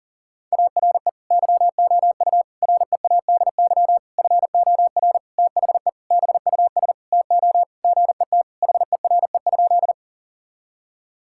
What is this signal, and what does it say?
Morse code, heard as a keyed tone, about 40 wpm, 700 Hz. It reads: ARE YOU READY FOR THE BUS TO GET HERE?
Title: Sentences from Top 1000 Words - Review Part 6 - 40wpm